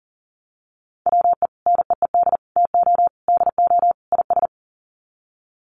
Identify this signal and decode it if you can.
Morse code, heard as a keyed tone, about 40 wpm, 700 Hz. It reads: WE NEED TO DO IS
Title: Sets of 5 Words Encore2 40wpm